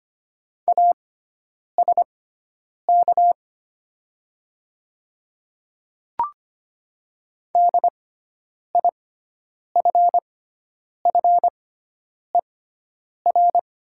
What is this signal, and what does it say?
Morse code, heard as a keyed tone, about 25 wpm, spaced as for 9 wpm, 700 Hz. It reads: ASK DIFFER
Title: 6x Character Spacing - Top 200 Words - 25wpm